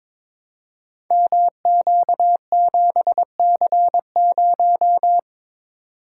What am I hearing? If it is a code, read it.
Morse code, heard as a keyed tone, about 22 wpm, 700 Hz. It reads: MQ7C0
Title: Code Groups - 5 Letter:Numbers - 22wpm